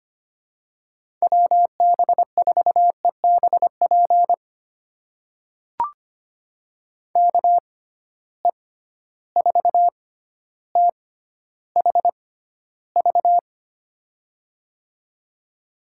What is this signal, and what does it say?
Morse code, heard as a keyed tone, about 25 wpm, spaced as for 13 wpm, 700 Hz. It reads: WB4EBP KE4THV
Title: USA Call Signs - 6 Characters - 6x Character Spacing - 25wpm